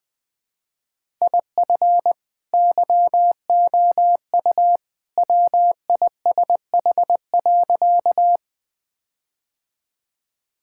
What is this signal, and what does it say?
Morse code, heard as a keyed tone, about 20 wpm, 700 Hz. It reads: IF YOU WISH.
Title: Sentences from Top 1600 Words 20wpm